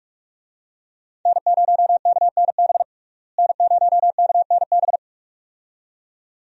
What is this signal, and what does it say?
Morse code, heard as a keyed tone, about 42 wpm, 700 Hz. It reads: N0KNB N0KNB
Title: USA Call Signs - 5 Characters - Repeated 2x - 45wpm